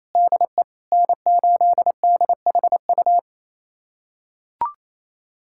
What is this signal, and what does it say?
Morse code, heard as a keyed tone, about 28 wpm, 700 Hz. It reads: DE N8DHU
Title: USA Callsign DE Callsign - 5 Characters - 28wpm